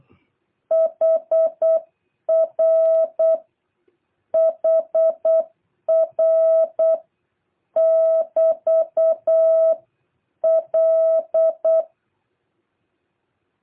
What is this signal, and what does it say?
Morse code, heard as a keyed tone, about 8 wpm, 660 Hz. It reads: HR HR <BT> L
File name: jh8cbh / 2023cw